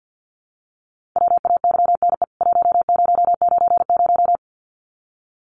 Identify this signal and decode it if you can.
Morse code, heard as a keyed tone, about 52 wpm, 700 Hz. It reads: WAYNE 1090
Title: CWT Exchange 50wpm